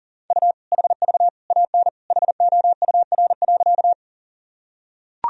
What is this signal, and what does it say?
Morse code, heard as a keyed tone, about 40 wpm, 700 Hz. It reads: U HV AN HOUR.
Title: Sentences from Top 400 Words and Abbreviations 40wpm